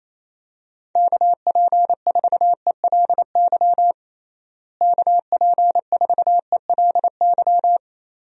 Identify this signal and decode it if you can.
Morse code, heard as a keyed tone, about 28 wpm, 700 Hz. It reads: KP4ELY KP4ELY
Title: USA Call Signs - 6 Characters - Repeated 3x - 28wpm